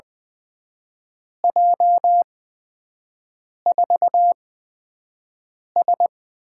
Morse code, heard as a keyed tone, about 21 wpm, spaced as for 5 wpm, 700 Hz. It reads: J4S
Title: USA Call Signs - 8x Character Spacing - 20wpm